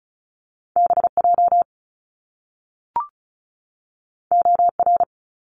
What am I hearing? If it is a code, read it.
Morse code, heard as a keyed tone, about 36 wpm, 700 Hz. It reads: BJ OR
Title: Top Two Letter Combinations in Words 35wpm